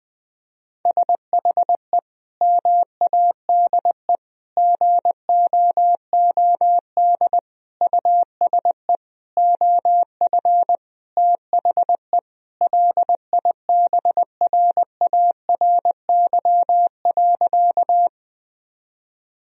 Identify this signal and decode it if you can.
Morse code, heard as a keyed tone, about 20 wpm, 700 Hz. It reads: SHE MADE GOOD USE OF THE LIBRARY.
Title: Sentences from Top 1000 Words - Review Part 4 - 20wpm